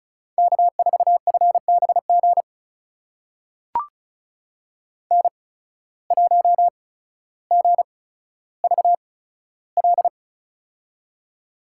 Morse code, heard as a keyed tone, about 36 wpm, spaced as for 15 wpm, 700 Hz. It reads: K4FBG N1GVL
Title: USA Call Signs - 5 Characters - 8x Character Spacing - 35wpm